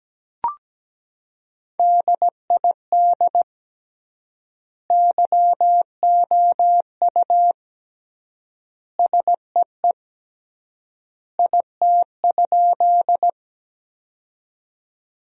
Morse code, heard as a keyed tone, about 17 wpm, 700 Hz. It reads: DID YOU SEE IT?
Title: 3x Word-Spacing - Sentences from Top 100 Words - 17wpm